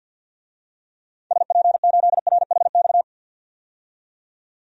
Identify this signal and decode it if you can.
Morse code, heard as a keyed tone, about 53 wpm, 700 Hz. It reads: SP8RHX
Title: DXCC-100 Call Signs - 6 Characters - 50wpm